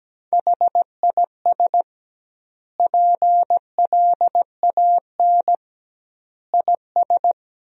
Morse code, heard as a keyed tone, about 17 wpm, 700 Hz. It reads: HIS PLAN IS
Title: 2x Word-Spacing - Sentences from Top 500 Words - 17wpm